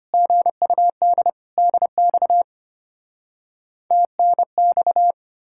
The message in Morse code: GUD DX TNX

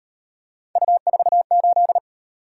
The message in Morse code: U48

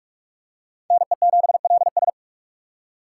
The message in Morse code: NE7LS

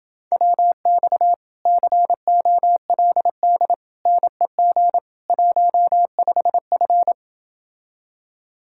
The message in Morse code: WX COLD NEG 15F

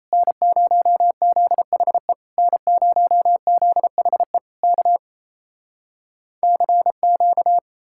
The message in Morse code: N0ZHE N0ZHE K CQ

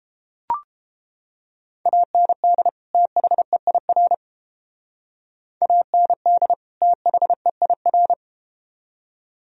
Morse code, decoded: AND THEIR AND THEIR

